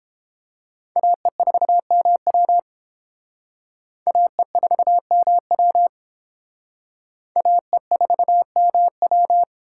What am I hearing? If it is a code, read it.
Morse code, heard as a keyed tone, about 29 wpm, 700 Hz. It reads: AE4MW AE4MW AE4MW